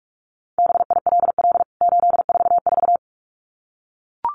Morse code, heard as a keyed tone, about 44 wpm, 700 Hz. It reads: BILL 844
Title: CWT Exchange 45wpm